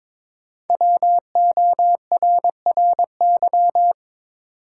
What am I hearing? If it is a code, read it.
Morse code, heard as a keyed tone, about 22 wpm, 700 Hz. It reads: WORRY
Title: Sentences from Top 1100 Words - New Word Drill - 22wpm